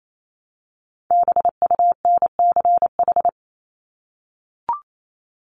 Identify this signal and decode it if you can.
Morse code, heard as a keyed tone, about 28 wpm, 700 Hz. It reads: BUNCH
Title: Top 1751-2000 Words 28wpm